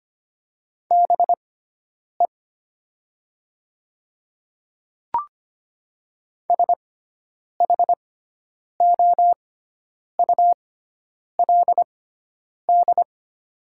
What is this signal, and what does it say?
Morse code, heard as a keyed tone, about 25 wpm, spaced as for 9 wpm, 700 Hz. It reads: BE SHOULD
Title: 6x Character Spacing - Top 300 Words - 25wpm